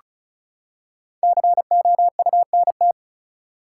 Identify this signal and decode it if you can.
Morse code, heard as a keyed tone, about 36 wpm, 700 Hz. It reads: COUNT
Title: Five Letter Words - 35wpm